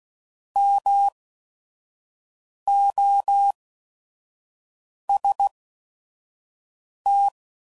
Morse code, heard as a keyed tone, about 16 wpm, spaced as for 5 wpm, 800 Hz. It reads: MOST
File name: CW Code Groups @5WPM / Technical Subjects / Gates Intro